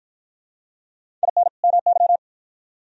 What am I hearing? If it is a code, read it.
Morse code, heard as a keyed tone, about 52 wpm, 700 Hz. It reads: IN MY